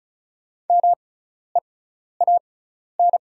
Morse code, heard as a keyed tone, about 36 wpm, spaced as for 12 wpm, 690 Hz. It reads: MEAN
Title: 6x Character Spacing - Top 300 Words - 35wpm